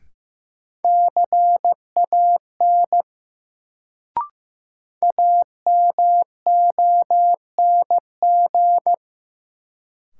Morse code, heard as a keyed tone, about 16 wpm, 700 Hz. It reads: CAN AMONG
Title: Sets of 2 Words - New Word Drill - 15wpm